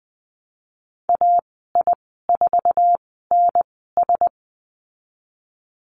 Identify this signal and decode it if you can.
Morse code, heard as a keyed tone, about 20 wpm, spaced as for 15 wpm, 700 Hz. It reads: AI4NS
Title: 173 Lesson 28g - Number 9 - Call Signs 20wpm